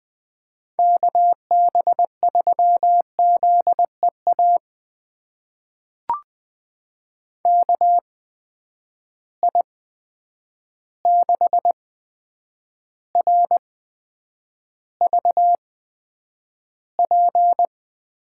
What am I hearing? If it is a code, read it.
Morse code, heard as a keyed tone, about 20 wpm, spaced as for 9 wpm, 700 Hz. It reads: KB3ZEA KI6RVP
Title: USA Call Signs - 6 Characters - 8x Character Spacing - 20wpm